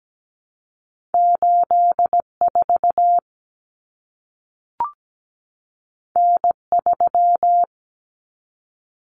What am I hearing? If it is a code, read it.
Morse code, heard as a keyed tone, about 17 wpm, 700 Hz. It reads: 84 N3
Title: Common Cut Numbers - 2-Digit 17wpm